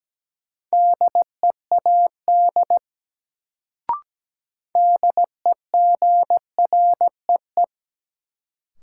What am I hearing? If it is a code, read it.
Morse code, heard as a keyed tone, about 17 wpm, 700 Hz. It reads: DEAD DEGREE
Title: Sentences from Top 1500 Words - New Word Drill - 17wpm